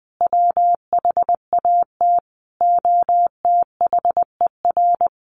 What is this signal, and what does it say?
Morse code, heard as a keyed tone, about 20 wpm, 700 Hz. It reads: WHAT OTHER
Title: Sentences from Top 600 Words 20wpm